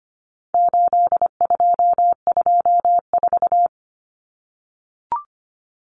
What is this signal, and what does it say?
Morse code, heard as a keyed tone, about 25 wpm, 700 Hz. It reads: 8224
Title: Common Cut Numbers - 4-Digit 25wpm